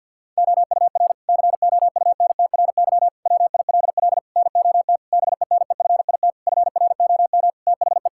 Morse code, heard as a keyed tone, about 52 wpm, 700 Hz. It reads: OUR COUNTRY WILL NOT BENEFIT FROM THE
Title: Sentences from Top 1000 Words - Review Part 6 - 50wpm